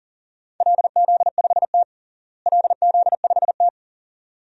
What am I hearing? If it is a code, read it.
Morse code, heard as a keyed tone, about 40 wpm, 700 Hz. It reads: LZ5T LZ5T
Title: DXCC-100 Call Signs - 4 Characters - Repeated 2x - 40wpm